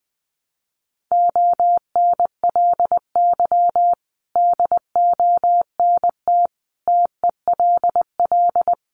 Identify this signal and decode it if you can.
Morse code, heard as a keyed tone, about 20 wpm, 700 Hz. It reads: ONLY DONT TELL
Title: Sentences from Top 900 Words 20wpm